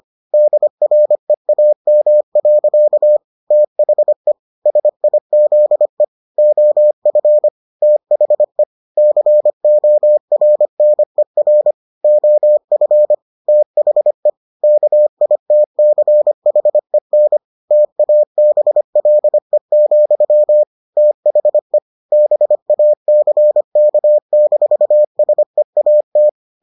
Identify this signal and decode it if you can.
Morse code, heard as a keyed tone, about 25 wpm, 600 Hz. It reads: DREAM. THE SIZE OF THE CORNER OF THE KITCHEN TABLE, THE BACK-SEAT